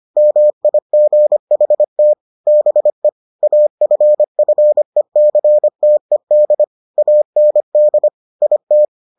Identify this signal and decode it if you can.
Morse code, heard as a keyed tone, about 25 wpm, 600 Hz. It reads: MIGHT BE AFFECTED AND IT